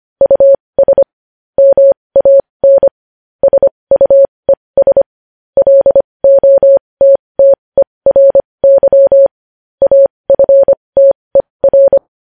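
Morse code, heard as a keyed tone, about 25 wpm, spaced as for 20 wpm, 550 Hz. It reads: US MAN SUES LOTTERY AFTER
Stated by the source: News headlines sent in Morse code at 20.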